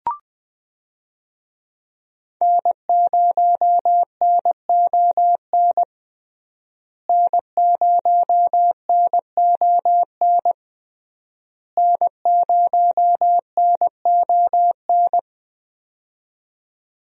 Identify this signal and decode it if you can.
Morse code, heard as a keyed tone, about 20 wpm, 700 Hz. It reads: N0NON N0NON N0NON
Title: USA Call Signs - 5 Characters - Repeated 3x - 20wpm